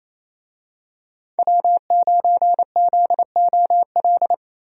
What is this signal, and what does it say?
Morse code, heard as a keyed tone, about 28 wpm, 700 Hz. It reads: W9ZOL
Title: USA Call Signs - 5 Characters - 28wpm